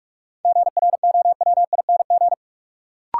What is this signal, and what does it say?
Morse code, heard as a keyed tone, about 46 wpm, 700 Hz. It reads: GROWING
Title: Sets of 5 Words Encore - New Word Drill - 45wpm